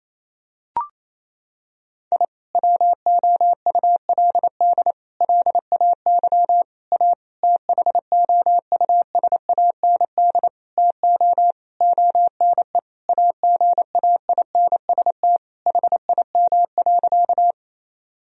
Morse code, heard as a keyed tone, about 28 wpm, 700 Hz. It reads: I WOULD LAY A THOUSAND TO ONE AGAINST HIM.
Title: Sentences from Top 1000 Words - Review Part 3 - 28wpm